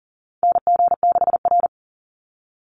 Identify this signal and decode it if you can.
Morse code, heard as a keyed tone, about 40 wpm, 700 Hz. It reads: NG6R
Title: NA SOTA Chaser Callsigns 40wpm